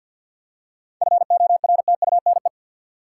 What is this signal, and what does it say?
Morse code, heard as a keyed tone, about 50 wpm, 700 Hz. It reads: FORTUNE
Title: Seven Letter Words - 50wpm